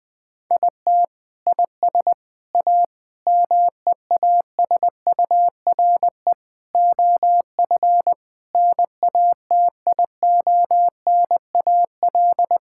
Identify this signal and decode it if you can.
Morse code, heard as a keyed tone, about 21 wpm, 700 Hz. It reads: IT IS A MEASURE OF NATIONAL
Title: Sentences from Top 1600 Words 20wpm